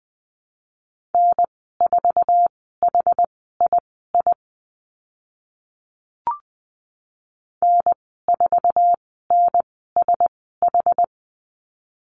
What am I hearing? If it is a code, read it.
Morse code, heard as a keyed tone, about 20 wpm, spaced as for 14 wpm, 700 Hz. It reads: N4HII N4NSH
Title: 069 Lesson 12g - Letter d - Call Signs 20wpm